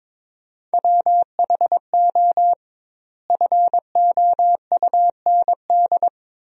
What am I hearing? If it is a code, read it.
Morse code, heard as a keyed tone, about 21 wpm, 700 Hz. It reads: WHO FOUND
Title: 2x Word-Spacing - Sentences from Top 300 Words - 22wpm